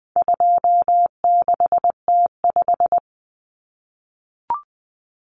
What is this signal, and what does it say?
Morse code, heard as a keyed tone, about 20 wpm, 700 Hz. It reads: 26T5
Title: Common Cut Numbers - 4-Digit 20wpm